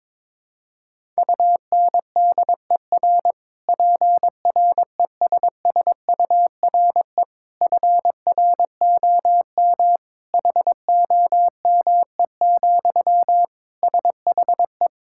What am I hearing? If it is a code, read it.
Morse code, heard as a keyed tone, about 22 wpm, 700 Hz. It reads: UNDER PRESSURE FROM HOME, SHE
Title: Sentences from Top 1000 Words - Review Part 6 - 22wpm